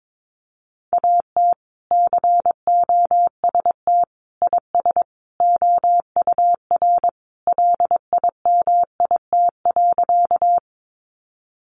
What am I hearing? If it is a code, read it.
Morse code, heard as a keyed tone, about 22 wpm, 700 Hz. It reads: AT COST IS OUR LIMIT.
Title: Sentences from Top 1500 Words 22wpm